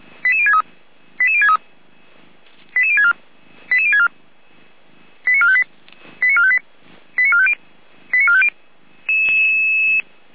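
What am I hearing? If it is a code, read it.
Five-tone selective call (ZVEI), heard as quick runs of five tones, tones of about 70 ms. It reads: 88063 88063 88064 88064 89458 89458 89450 89450
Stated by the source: ZVEI five-tone selective call received on SDR.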